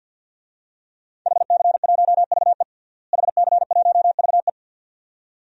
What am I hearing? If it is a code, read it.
Morse code, heard as a keyed tone, about 52 wpm, 700 Hz. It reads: HC1VE HC1VE
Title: DXCC-100 Call Signs - 5 Characters - Repeated 2x - 50wpm